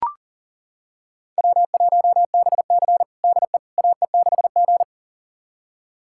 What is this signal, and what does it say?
Morse code, heard as a keyed tone, about 40 wpm, 700 Hz. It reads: W1BC DE AE6G
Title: USA Callsign DE Callsign - 4 Characters - 40wpm